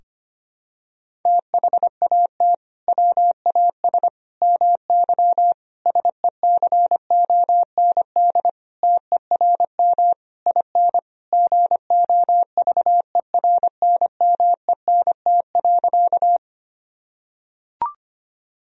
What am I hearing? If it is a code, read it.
Morse code, heard as a keyed tone, about 25 wpm, 700 Hz. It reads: THAT WAS MY SECOND TERM IN GOVERNMENT.
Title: Sentences from Top 1000 Words - Review Part 4 - 25wpm